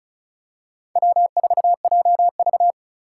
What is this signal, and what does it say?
Morse code, heard as a keyed tone, about 36 wpm, 700 Hz. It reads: W4JV
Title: USA Call Signs - 4 Characters - 4x Character Spacing - 35wpm